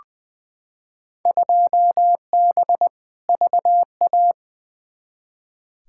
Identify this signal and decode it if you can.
Morse code, heard as a keyed tone, about 20 wpm, 700 Hz. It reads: 2B VA
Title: Field Day Exchanges Encore - 20wpm